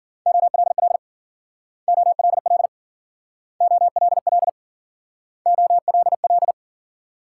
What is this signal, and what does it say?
Morse code, heard as a keyed tone, about 51 wpm, 700 Hz. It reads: OLL OLL OLL OLL